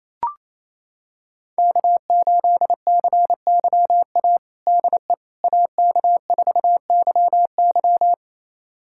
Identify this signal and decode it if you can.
Morse code, heard as a keyed tone, about 28 wpm, 700 Hz. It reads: K8CYA DE AK4YY
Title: USA Callsign DE Callsign - 5 Characters - 28wpm